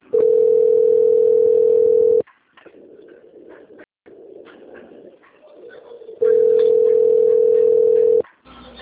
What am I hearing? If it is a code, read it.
Ringback tone (North America), 440 + 480 Hz.
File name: out- - - - - .23794